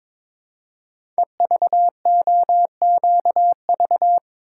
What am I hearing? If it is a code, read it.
Morse code, heard as a keyed tone, about 22 wpm, 700 Hz. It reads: EVOQV
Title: Code Groups - 5 Letters - 22wpm